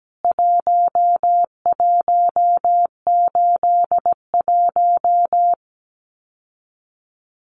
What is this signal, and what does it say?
Morse code, heard as a keyed tone, about 17 wpm, 700 Hz. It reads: 1181
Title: CWT Exchange 17wpm